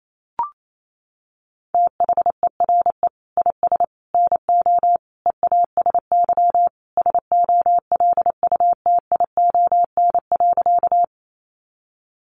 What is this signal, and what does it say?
Morse code, heard as a keyed tone, about 28 wpm, 700 Hz. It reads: THERE IS NO EASY SOLUTION.
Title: Sentences from Top 1300 Words 28wpm